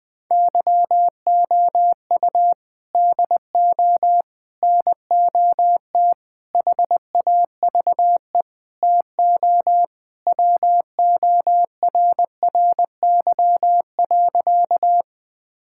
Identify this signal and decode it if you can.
Morse code, heard as a keyed tone, about 20 wpm, 700 Hz. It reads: YOU DO NOT HAVE TO WORRY.
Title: Sentences with 5-Word Phrases - Spoken Beginning - 20wpm